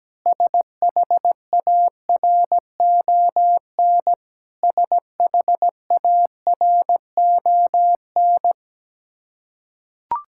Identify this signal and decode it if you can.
Morse code, heard as a keyed tone, about 17 wpm, 700 Hz. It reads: SHARON SHARON
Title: K1USN SST QSO Practice 17wpm